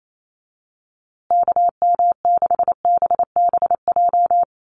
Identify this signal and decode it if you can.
Morse code, heard as a keyed tone, about 28 wpm, 700 Hz. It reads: KM6BBJ
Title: USA Call Signs Repeated 3x 28wpm